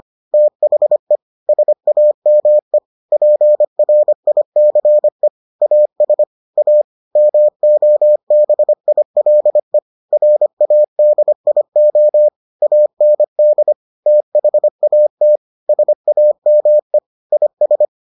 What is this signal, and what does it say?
Morse code, heard as a keyed tone, about 25 wpm, 600 Hz. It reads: THE SAME PRICE AS A MOBILE RADIO AND THAT SAME IS